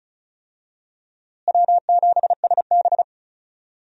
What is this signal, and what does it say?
Morse code, heard as a keyed tone, about 34 wpm, 700 Hz. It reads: W7SB
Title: USA Call Signs - 4 Characters - Repeated 2x - 35wpm